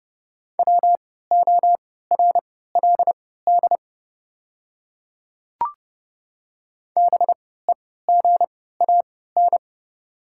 Morse code, heard as a keyed tone, about 30 wpm, spaced as for 17 wpm, 700 Hz. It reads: WORLD BEGAN
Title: 3x Character Spacing - Top 300 Words - 30wpm